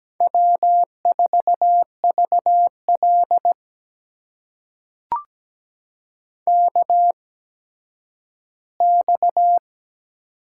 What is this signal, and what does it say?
Morse code, heard as a keyed tone, about 17 wpm, spaced as for 10 wpm, 700 Hz. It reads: W4VL KX
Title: USA Call Signs - 4 Characters - 8x Character Spacing - 17wpm